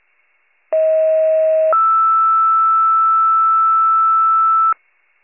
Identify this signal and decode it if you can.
Two-tone page, heard as a one-second tone, then a three-second tone, about 640 then 1320 Hz. It is heard from 0.7 to 4.7 s.